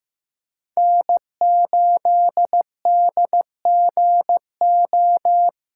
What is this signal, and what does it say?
Morse code, heard as a keyed tone, about 15 wpm, 700 Hz. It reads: N8DGO
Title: USA Callsign DE Callsign - 5 Characters - 15wpm